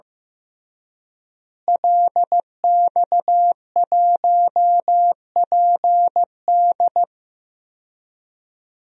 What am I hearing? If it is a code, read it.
Morse code, heard as a keyed tone, about 15 wpm, 700 Hz. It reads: LX1PD
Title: DXCC-100 Call Signs - 5 Characters - Repeated 3x - 15wpm